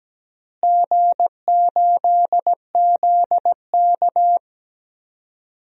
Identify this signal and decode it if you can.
Morse code, heard as a keyed tone, about 17 wpm, 700 Hz. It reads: G8ZK
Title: DXCC-100 Call Signs - 4 Characters - Repeated 2x - 17wpm